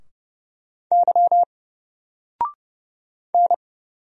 Morse code, heard as a keyed tone, about 30 wpm, 700 Hz. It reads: Y N
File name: Single-Letter-Number-30wpm